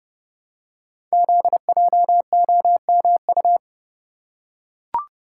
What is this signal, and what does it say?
Morse code, heard as a keyed tone, about 30 wpm, 700 Hz. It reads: ZJOMU
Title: Code Groups - 5 Letters - 30wpm